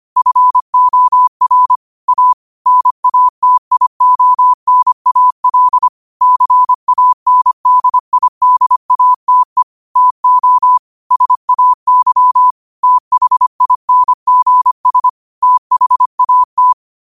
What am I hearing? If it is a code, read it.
Morse code, heard as a keyed tone, about 25 wpm, 1000 Hz. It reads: FOR A NATIONAL CANDIDATE TO SAY THINGS THAT
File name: qotdfiles / 25wpm